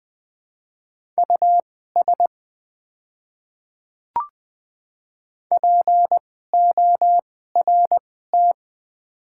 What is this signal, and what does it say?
Morse code, heard as a keyed tone, about 21 wpm, spaced as for 15 wpm, 700 Hz. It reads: US PORT
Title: 2x Character Spacing - Top 200 Words - 20wpm